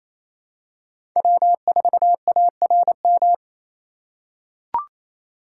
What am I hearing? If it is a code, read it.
Morse code, heard as a keyed tone, about 27 wpm, 700 Hz. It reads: W4ARM
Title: USA Call Signs - 5 Characters - 2x Character Spacing - 28wpm